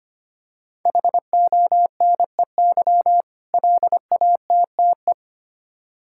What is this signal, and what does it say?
Morse code, heard as a keyed tone, about 25 wpm, 700 Hz. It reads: HONEY LATTE
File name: practice / 25wpm / Difficult Words with Hints - 25wpm